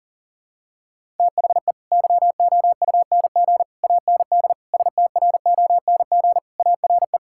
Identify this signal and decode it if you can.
Morse code, heard as a keyed tone, about 37 wpm, 700 Hz. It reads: THE YOUNG AND STRONG ARE
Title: Sentences from Top 1000 Words - Review Part 5 - 40wpm